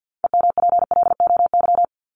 Morse code, heard as a keyed tone, about 50 wpm, 700 Hz. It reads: EMPLOY
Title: Top 1751-2000 Words 50wpm